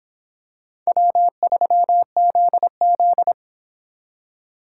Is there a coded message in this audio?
Yes: Morse code from 0.9 to 3.3 s, heard as a keyed tone, about 26 wpm, 700 Hz. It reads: W3ZZ